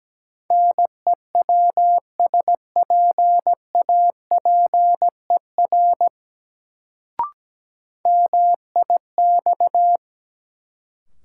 Morse code, heard as a keyed tone, about 17 wpm, 700 Hz. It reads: NEWSPAPER MIX
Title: Sentences from Top 1000 Words - Review Part 4 - New Word Drill - 17wpm